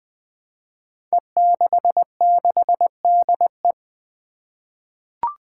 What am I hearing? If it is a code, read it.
Morse code, heard as a keyed tone, about 20 wpm, 700 Hz. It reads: E66DE
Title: Five Challenging Letter-Number Combinations 20wpm